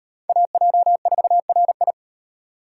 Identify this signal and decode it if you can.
Morse code, heard as a keyed tone, about 38 wpm, 700 Hz. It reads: AJ4RI